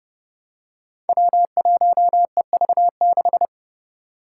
Morse code, heard as a keyed tone, about 30 wpm, 700 Hz. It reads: W1EV6
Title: Five Challenging Letter-Number Combinations 30wpm